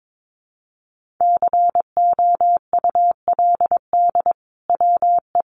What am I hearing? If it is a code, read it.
Morse code, heard as a keyed tone, about 22 wpm, 700 Hz. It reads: COULD WE
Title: Sentences from Top 1200 Words 22wpm